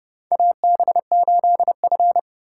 Morse code, heard as a keyed tone, about 29 wpm, 700 Hz. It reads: AB8F
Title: USA Call Signs - 4x Character Spacing - 30wpm